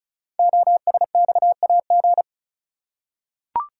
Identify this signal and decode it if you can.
Morse code, heard as a keyed tone, about 36 wpm, 690 Hz. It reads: OSXAG
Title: Code Groups - 5 Letters - 35wpm